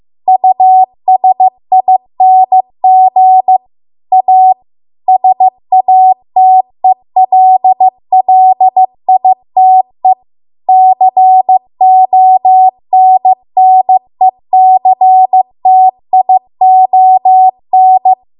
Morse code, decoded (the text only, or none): USING A SATELLITE CONNECTION